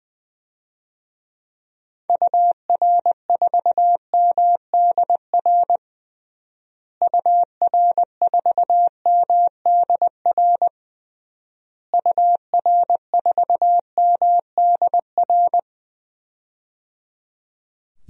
Morse code, heard as a keyed tone, about 20 wpm, 700 Hz. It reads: UR4MDR UR4MDR UR4MDR
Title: DXCC-100 Call Signs - 6 Characters - Repeated 3x - 20wpm